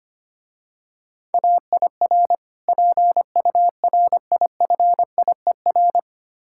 Morse code, heard as a keyed tone, about 25 wpm, 700 Hz. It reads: AIR PURIFIER
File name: practice / 25wpm / Difficult Words - Speed-Racing with Hints - 25wpm